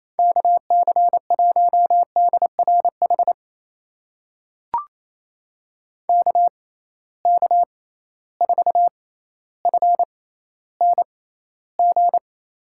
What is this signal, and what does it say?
Morse code, heard as a keyed tone, about 28 wpm, spaced as for 14 wpm, 700 Hz. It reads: KC1DRH KK4FNG
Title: USA Call Signs - 6x Character Spacing - 28wpm